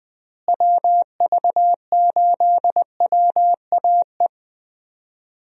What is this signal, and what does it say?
Morse code, heard as a keyed tone, about 20 wpm, 700 Hz. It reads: WV8WAE
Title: USA Call Signs - 6 Characters - Repeated 2x - 20wpm